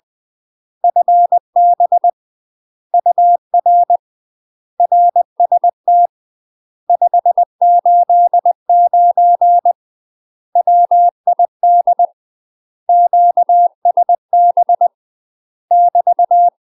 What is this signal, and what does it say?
Morse code, heard as a keyed tone, about 20 wpm, 700 Hz. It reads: FB UR RST 589 WID QSB <BT>
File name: UK MASTER QSOs TRAINING / 20WPM / UK MASTER QSOs 0036